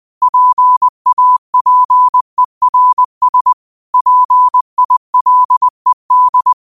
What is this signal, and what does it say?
Morse code, heard as a keyed tone, about 20 wpm, 1000 Hz. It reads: PAPERS PILED